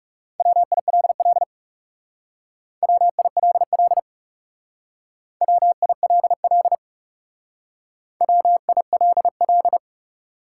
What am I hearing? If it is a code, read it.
Morse code, heard as a keyed tone, about 39 wpm, 700 Hz. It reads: WILL WILL WILL WILL